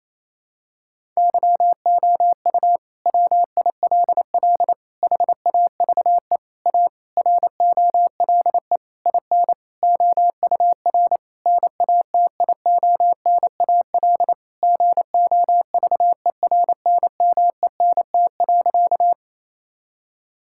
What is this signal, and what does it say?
Morse code, heard as a keyed tone, about 27 wpm, 700 Hz. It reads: YOU WILL HAVE A ROLE IN OUR NATIONAL GOVERNMENT.
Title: Sentences from Top 1000 Words - Review Part 6 - 28wpm